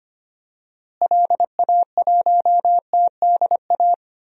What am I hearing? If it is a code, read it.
Morse code, heard as a keyed tone, about 25 wpm, 700 Hz. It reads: LA1TDA
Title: DXCC-100 Call Signs - 6 Characters - Repeated 2x - 25wpm